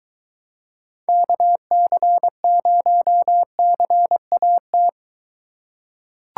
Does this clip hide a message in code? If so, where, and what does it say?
Yes: Morse code from 1.1 to 4.9 s, heard as a keyed tone, about 22 wpm, 700 Hz. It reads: KC0CAT